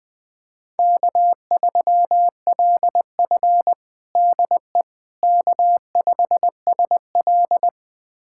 Morse code, heard as a keyed tone, about 20 wpm, 700 Hz. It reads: K3LF DE K5SL
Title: USA Callsign DE Callsign - 4 Characters - 20wpm